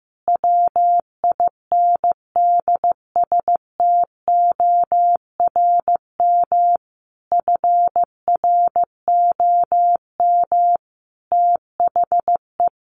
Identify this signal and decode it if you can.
Morse code, heard as a keyed tone, about 15 wpm, 700 Hz. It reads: WINDSTORM FROM THE
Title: A Sound Of Thunder 15wpm